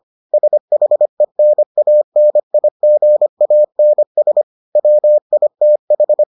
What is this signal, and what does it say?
Morse code, heard as a keyed tone, about 25 wpm, 600 Hz. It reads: SHENANIGANS WITH